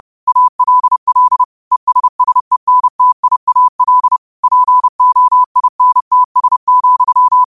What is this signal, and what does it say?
Morse code, heard as a keyed tone, about 30 wpm, 1000 Hz. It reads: ALL ESSENTIAL POINTS,
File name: qotdfiles / 30wpm